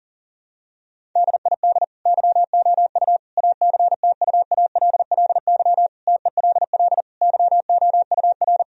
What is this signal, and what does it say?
Morse code, heard as a keyed tone, about 40 wpm, 700 Hz. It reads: DID YOU ACTUALLY TELL YOUR
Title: Sentences from Top 1000 Words - Review Part 4 - 40wpm